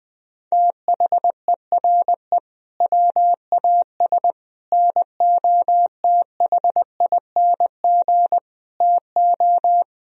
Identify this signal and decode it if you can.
Morse code, heard as a keyed tone, about 20 wpm, 700 Hz. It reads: THERE WAS NOTHING TO
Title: Sentences from Top 1000 Words - Review Part 1 - 20wpm